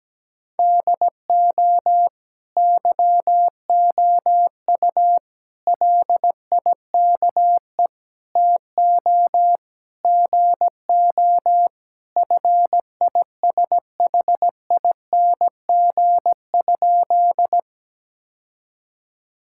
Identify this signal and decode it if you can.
Morse code, heard as a keyed tone, about 17 wpm, 700 Hz. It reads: DO YOU LIKE TO GO FISHING?
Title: Sentences from Top 1000 Words - Review Part 5 - 17wpm